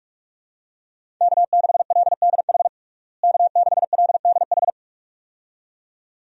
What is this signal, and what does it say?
Morse code, heard as a keyed tone, about 46 wpm, 700 Hz. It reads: K6LDH K6LDH
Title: USA Call Signs - 5 Characters - Repeated 2x - 45wpm